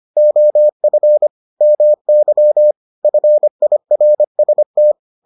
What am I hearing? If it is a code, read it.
Morse code, heard as a keyed tone, about 25 wpm, 600 Hz. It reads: OF MY FIRST